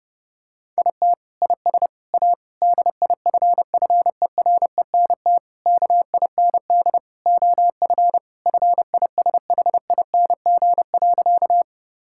Morse code, heard as a keyed tone, about 32 wpm, 700 Hz. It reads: IT IS A DIFFERENT KIND OF FISHING.
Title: Sentences from Top 1000 Words - Review Part 5 - 30wpm